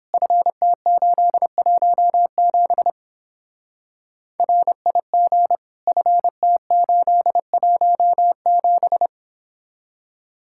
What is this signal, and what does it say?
Morse code, heard as a keyed tone, about 31 wpm, 700 Hz. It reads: FT817 RIG FT817